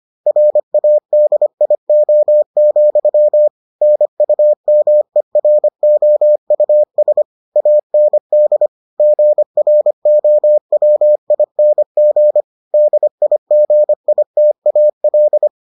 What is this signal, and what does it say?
Morse code, heard as a keyed tone, about 25 wpm, 600 Hz. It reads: RADIO, NUMEROUS AND GROWING DIGITAL